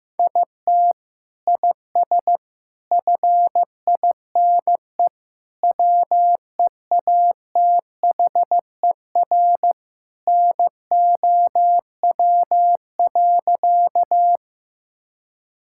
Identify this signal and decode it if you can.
Morse code, heard as a keyed tone, about 15 wpm, 700 Hz. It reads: IT IS FINE WEATHER NOW.
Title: Sentences from Top 1000 Words - Review Part 5 - 15wpm